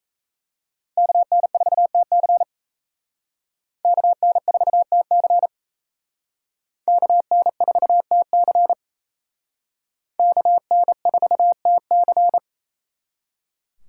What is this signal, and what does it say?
Morse code, heard as a keyed tone, about 35 wpm, 700 Hz. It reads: KN4TC KN4TC KN4TC KN4TC
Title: Speed Racing - USA 2x2 Call Signs 28wpm